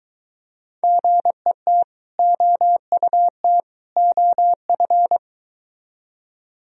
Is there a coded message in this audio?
Yes: Morse code from 0.8 to 5.2 s, heard as a keyed tone, about 23 wpm, 700 Hz. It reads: GET OUT OF